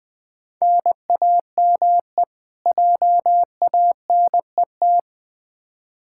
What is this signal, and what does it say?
Morse code, heard as a keyed tone, about 20 wpm, 700 Hz. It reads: NAME JANET
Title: K1USN SST QSO Practice 20wpm